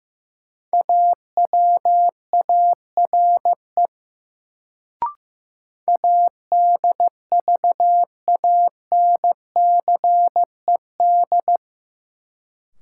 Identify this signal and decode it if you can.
Morse code, heard as a keyed tone, about 15 wpm, 700 Hz. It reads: AWARE ADVANCED
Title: Sentences from Top 1200 Words - New Word Drill Alt - 15wpm